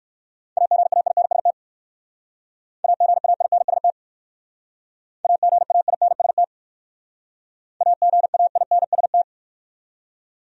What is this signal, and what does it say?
Morse code, heard as a keyed tone, about 62 wpm, 700 Hz. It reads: AGAINST AGAINST AGAINST AGAINST